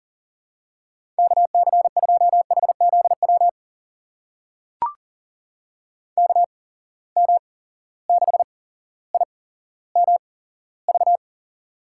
Morse code, heard as a keyed tone, about 40 wpm, spaced as for 17 wpm, 700 Hz. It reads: KC2HZW KM6IMV
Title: USA Call Signs - 6 Characters - 8x Character Spacing - 40wpm